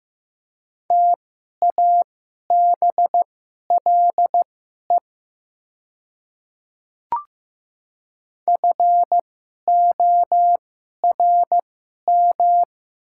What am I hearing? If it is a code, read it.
Morse code, heard as a keyed tone, about 15 wpm, spaced as for 11 wpm, 700 Hz. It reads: TABLE FORM
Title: 2x Character Spacing - Top 500 Words - 15wpm